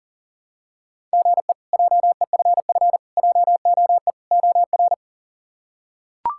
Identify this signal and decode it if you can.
Morse code, heard as a keyed tone, about 37 wpm, 700 Hz. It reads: GE JEFF JOE OR
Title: K1USN SST QSO Practice - 40wpm